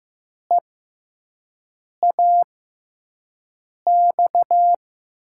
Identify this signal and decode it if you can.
Morse code, heard as a keyed tone, about 15 wpm, spaced as for 5 wpm, 700 Hz. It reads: EAX
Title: USA Call Signs - 6 Characters - 6x Character Spacing - 15wpm